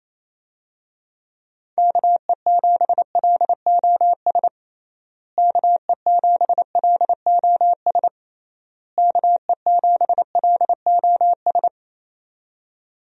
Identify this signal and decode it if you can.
Morse code, heard as a keyed tone, about 28 wpm, 700 Hz. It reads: KE7LOS KE7LOS KE7LOS
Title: USA Call Signs - 6 Characters - Repeated 3x - 28wpm